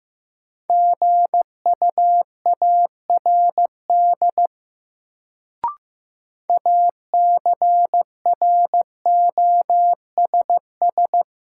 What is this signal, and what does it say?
Morse code, heard as a keyed tone, about 15 wpm, 700 Hz. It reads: GUARD ACROSS
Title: Sentences from Top 700 Words - New Word Drill - 15wpm